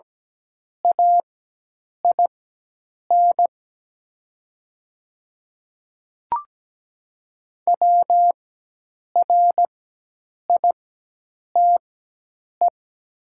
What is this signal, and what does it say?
Morse code, heard as a keyed tone, about 17 wpm, spaced as for 8 wpm, 700 Hz. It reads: AIN WRITE
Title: 4x Character Spacing - Top 500 Words - 17wpm